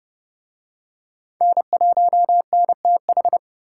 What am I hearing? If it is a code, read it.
Morse code, heard as a keyed tone, about 29 wpm, 700 Hz. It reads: N1NTH